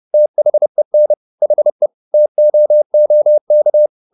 Morse code, heard as a keyed tone, about 30 wpm, 600 Hz. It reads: THEN HE TOOK